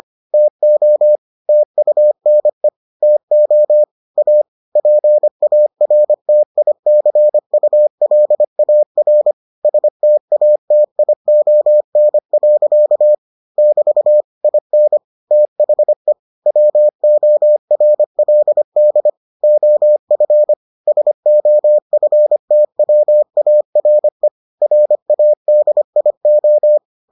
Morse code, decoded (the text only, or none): TO TUNE TO A PARTICULAR STATION. <BT> IN THE WORLD OF SOFTWARE RADIO